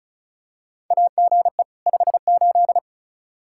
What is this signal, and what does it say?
Morse code, heard as a keyed tone, about 36 wpm, 700 Hz. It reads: AGE 58
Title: Practice QSO - Exchange 3 - More Info - 35wpm